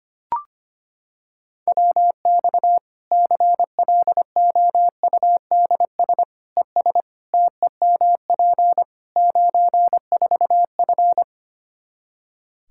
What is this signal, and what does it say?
Morse code, heard as a keyed tone, about 25 wpm, 700 Hz. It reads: WX CLOUDS ES TEMP 94F
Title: K1USN SST QSO Practice 25wpm